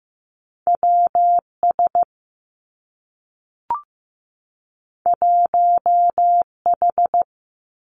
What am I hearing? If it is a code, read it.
Morse code, heard as a keyed tone, about 15 wpm, 700 Hz. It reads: W S 1 H
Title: Two Challenging Letter-Number Combinations 15wpm